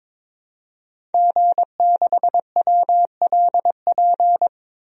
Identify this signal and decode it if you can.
Morse code, heard as a keyed tone, about 22 wpm, 700 Hz. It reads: G6WLP
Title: DXCC-100 Call Signs - 5 Characters - Repeated 2x - 22wpm